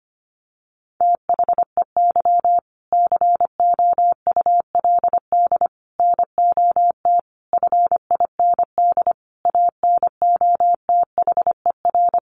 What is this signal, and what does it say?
Morse code, heard as a keyed tone, about 25 wpm, 700 Hz. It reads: THEY COULD NOT FIND ANOTHER
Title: Sentences from Top 1400 Words 25wpm